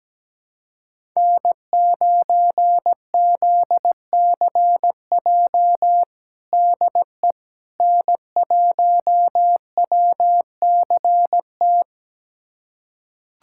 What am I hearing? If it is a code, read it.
Morse code, heard as a keyed tone, about 17 wpm, 700 Hz. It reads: N9ZCJ DE N1WCT
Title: USA Callsign DE Callsign - 5 Characters - 17wpm